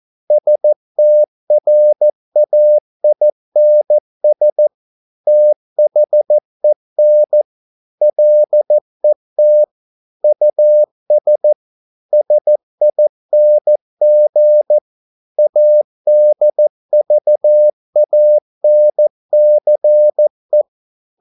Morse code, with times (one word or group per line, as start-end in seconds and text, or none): 0.3-4.7 STRAINS
5.3-7.4 THEN
8.0-9.6 LET
10.2-11.5 US
12.1-14.8 SING
15.4-20.6 ADVANCE